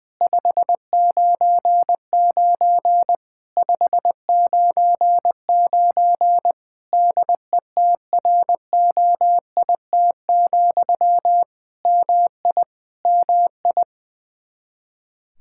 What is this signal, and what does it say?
Morse code, heard as a keyed tone, about 20 wpm, 700 Hz. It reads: 599 599 DETROIT, MI MI
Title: Practice QSO - Exchange 1 - Beginning - 20wpm